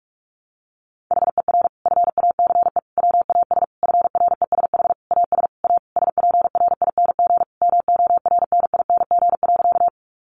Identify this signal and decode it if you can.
Morse code, heard as a keyed tone, about 44 wpm, 700 Hz. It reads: HER FACE WAS FRESH AS A SPRING MORNING.
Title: Sentences from Top 1200 Words 45wpm